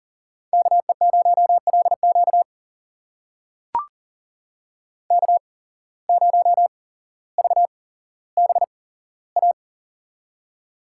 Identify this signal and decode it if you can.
Morse code, heard as a keyed tone, about 42 wpm, spaced as for 17 wpm, 700 Hz. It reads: KE0LQ K0VBA
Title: USA Call Signs - 5 Characters - 8x Character Spacing - 40wpm